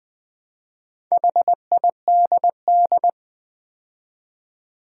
Morse code, heard as a keyed tone, about 20 wpm, 700 Hz. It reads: HIDD
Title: Build A Word - 2 Letters At A Time - 6-Letter Words - 20wpm